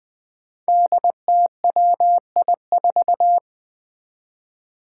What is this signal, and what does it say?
Morse code, heard as a keyed tone, about 21 wpm, 700 Hz. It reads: DTWI4
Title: Five Challenging Letter-Number Combinations 20wpm